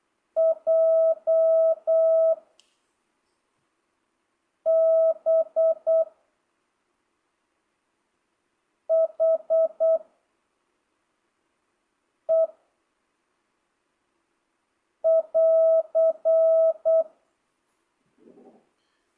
Morse code, heard as a keyed tone, about 8 wpm, 660 Hz. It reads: J B H E <AR>
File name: jh8cbh / 2025cw